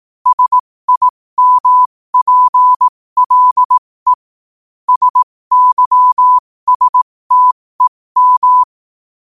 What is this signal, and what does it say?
Morse code, heard as a keyed tone, about 18 wpm, spaced as for 15 wpm, 1000 Hz. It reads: SIMPLE SYSTEM
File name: qotdfiles / 15wpm